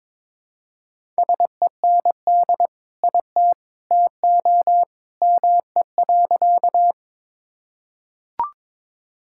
Morse code, heard as a keyed tone, about 21 wpm, 700 Hz. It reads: SEND IT TO ME.
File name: practice / 22wpm / Sentences with 2-Word Phrases - 22wpm